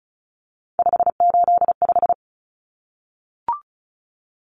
Morse code, heard as a keyed tone, about 34 wpm, 700 Hz. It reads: 585
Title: Numbers - 3-Digit 35wpm